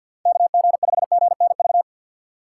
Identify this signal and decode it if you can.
Morse code, heard as a keyed tone, about 52 wpm, 700 Hz. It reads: KG5GNV
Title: USA Call Signs - 2x Character Spacing - 50wpm